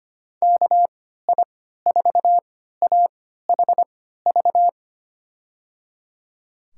Morse code, heard as a keyed tone, about 25 wpm, spaced as for 14 wpm, 700 Hz. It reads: KI4AHV
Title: USA Call Signs - 6 Characters - 3x Character Spacing - 25wpm